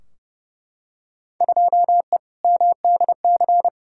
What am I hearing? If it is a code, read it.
Morse code, heard as a keyed tone, about 30 wpm, 700 Hz. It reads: 2E MDC
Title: Field Day Exchanges - 30wpm